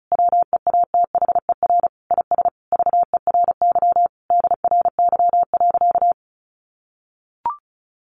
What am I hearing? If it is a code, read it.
Morse code, heard as a keyed tone, about 34 wpm, 700 Hz. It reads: WEATHER IS VERY DRY.
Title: Sentences from Top 1400 Words 35wpm